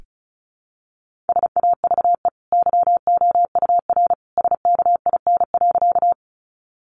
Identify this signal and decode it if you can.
Morse code, heard as a keyed tone, about 34 wpm, 700 Hz. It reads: SAVE YOUR SKIN.
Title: Sentences from Top 1300 Words 35wpm